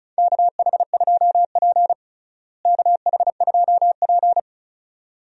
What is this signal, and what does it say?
Morse code, heard as a keyed tone, about 36 wpm, 700 Hz. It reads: KH2P KH2P
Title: USA Call Signs - 4 Characters - Repeated 2x - 35wpm